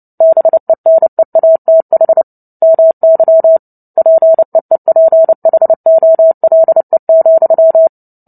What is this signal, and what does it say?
Morse code, heard as a keyed tone, about 30 wpm, 650 Hz. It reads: BENEATH MY PEEPHOLE,
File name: Book Two / 03-Chapter 3 - 29 WPM